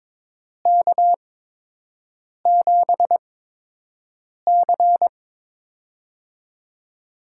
Morse code, heard as a keyed tone, about 23 wpm, spaced as for 6 wpm, 700 Hz. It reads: K7C
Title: USA Call Signs - 4 Characters - 8x Character Spacing - 22wpm